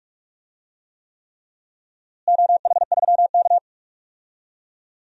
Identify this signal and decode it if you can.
Morse code, heard as a keyed tone, about 46 wpm, 700 Hz. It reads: OH3K
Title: DXCC-100 Call Signs - 4 Characters - Repeated 3x - 45wpm